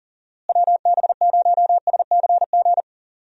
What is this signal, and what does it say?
Morse code, heard as a keyed tone, about 38 wpm, 700 Hz. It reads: WB0SCG
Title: USA Call Signs - 6 Characters - Repeated 2x - 40wpm